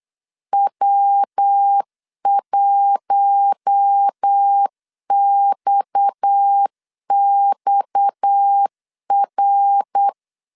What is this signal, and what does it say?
Morse code, heard as a keyed tone, about 8 wpm, 800 Hz. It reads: W1XXR
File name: LEZIONI / 6 PKX0 / 6 PKX 8wpm 3041